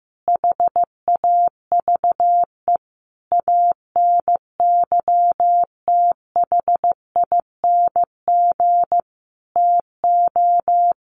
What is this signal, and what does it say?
Morse code, heard as a keyed tone, about 15 wpm, 700 Hz. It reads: HAVE ANYTHING TO
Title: Sentences from Top 1000 Words 15wpm